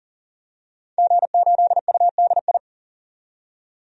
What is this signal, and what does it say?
Morse code, heard as a keyed tone, about 40 wpm, 700 Hz. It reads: G8UDI
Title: DXCC-100 Call Signs - 5 Characters - 40wpm